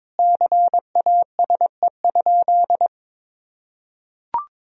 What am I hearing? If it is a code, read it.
Morse code, heard as a keyed tone, about 22 wpm, 700 Hz. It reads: CASE?
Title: Sentences from Top 1000 Words - Review Part 4 - 22wpm